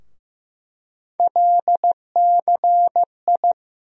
Morse code, heard as a keyed tone, about 15 wpm, 700 Hz. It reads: LCI
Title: USA Call Sign Postfixes - 3 Letters 15wpm